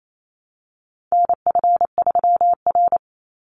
Morse code, heard as a keyed tone, about 28 wpm, 700 Hz. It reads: NF3R
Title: Call Signs - 28wpm